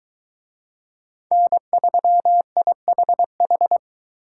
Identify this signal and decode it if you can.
Morse code, heard as a keyed tone, about 22 wpm, 700 Hz. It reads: N3IHH